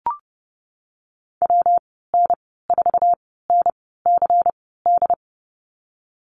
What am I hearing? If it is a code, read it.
Morse code, heard as a keyed tone, about 30 wpm, spaced as for 17 wpm, 700 Hz. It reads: WN4NCD
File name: Lesson 19g - Letter w - Call Signs 30wpm